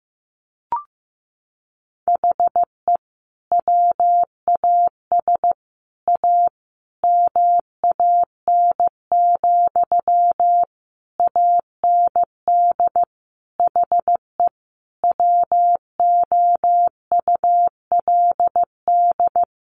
Morse code, heard as a keyed tone, about 15 wpm, 700 Hz. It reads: HE WAS A MAN, AND HE WOULD
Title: Sentences from Top 1500 Words 15wpm